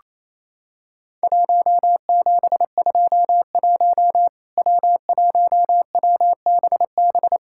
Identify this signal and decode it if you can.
Morse code, heard as a keyed tone, about 28 wpm, 700 Hz. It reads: 1721 W1WBB
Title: Worked All Europe DX Contest - QTC Exchange - 28wpm